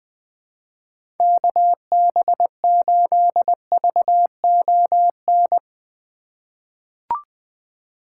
Morse code, heard as a keyed tone, about 20 wpm, 700 Hz. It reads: KB8VON
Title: 2023 POTA Hunter Call Signs - 20wpm